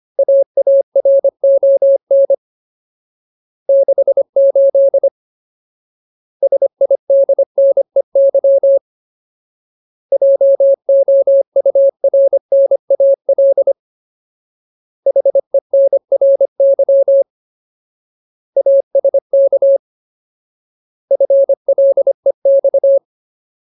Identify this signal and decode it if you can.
Morse code, heard as a keyed tone, about 25 wpm, 550 Hz. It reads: AARON 68 SIDNEY JOURNAL HENRY ASK FLEX
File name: Ultimate Word List / 25WPM Ultimate